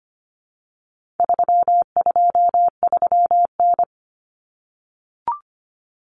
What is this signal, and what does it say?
Morse code, heard as a keyed tone, about 25 wpm, 700 Hz. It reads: 323N
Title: Common Cut Numbers - 4-Digit 25wpm